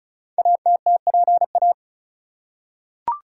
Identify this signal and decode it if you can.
Morse code, heard as a keyed tone, about 36 wpm, 700 Hz. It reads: ATTPA
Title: Code Groups - 5 Letters - 35wpm